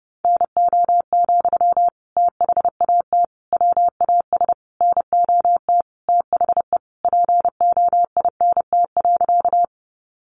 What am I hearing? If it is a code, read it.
Morse code, heard as a keyed tone, about 30 wpm, 700 Hz. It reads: NO, THAT WAS NOT THE POINT.
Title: Sentences from Top 600 Words 30wpm